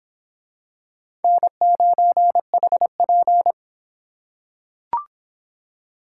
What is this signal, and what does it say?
Morse code, heard as a keyed tone, about 24 wpm, 700 Hz. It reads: N9HP